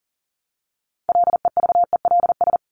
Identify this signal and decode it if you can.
Morse code, heard as a keyed tone, about 40 wpm, 700 Hz. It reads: LEVELS
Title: ES:S Suffix - Top 100 Words with Base Words 40wpm